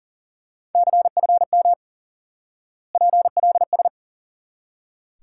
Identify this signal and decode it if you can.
Morse code, heard as a keyed tone, about 40 wpm, 700 Hz. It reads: CFM PLS
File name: IQE Course 3 - Lesson 89 - YAGI - Two Elements - 40wpm